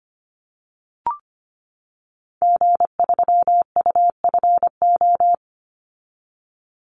Morse code, heard as a keyed tone, about 25 wpm, 700 Hz. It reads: G3UFO
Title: Call Signs - 25wpm